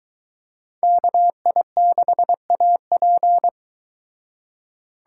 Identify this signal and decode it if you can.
Morse code, heard as a keyed tone, about 22 wpm, 700 Hz. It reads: KI6AP